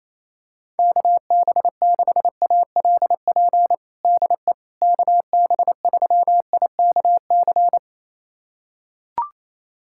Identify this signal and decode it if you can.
Morse code, heard as a keyed tone, about 28 wpm, 700 Hz. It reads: KB6ALP DE KB3IKC
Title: USA Callsign DE Callsign 28wpm